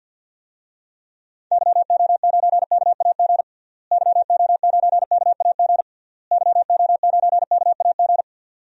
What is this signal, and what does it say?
Morse code, heard as a keyed tone, about 51 wpm, 700 Hz. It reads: YO9KAG YO9KAG YO9KAG
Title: DXCC-100 Call Signs - 6 Characters - Repeated 3x - 50wpm